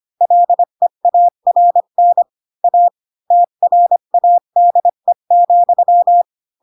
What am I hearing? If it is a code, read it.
Morse code, heard as a keyed tone, about 25 wpm, 700 Hz. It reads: LEARN A TRADE,
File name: Pinocchio / Pinocchio ch4 20 25 ik1hge